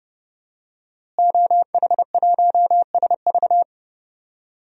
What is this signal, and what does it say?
Morse code, heard as a keyed tone, about 31 wpm, 700 Hz. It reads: OH1SV
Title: DXCC-100 Call Signs - 5 Characters - Repeated 2x - 30wpm